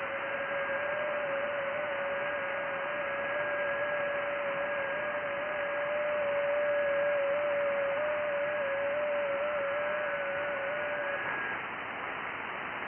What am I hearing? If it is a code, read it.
FT8.
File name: IC-7100